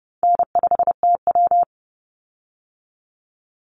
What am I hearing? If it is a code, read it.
Morse code, heard as a keyed tone, about 28 wpm, 700 Hz. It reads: N5TW
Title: Easy Call Signs Repeated 3x - 30wpm